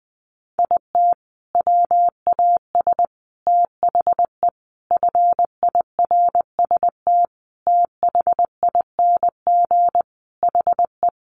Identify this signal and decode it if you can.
Morse code, heard as a keyed tone, about 20 wpm, 700 Hz. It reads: IT WAS THE FIRST THING HE
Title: Sentences from Top 300 Words 20wpm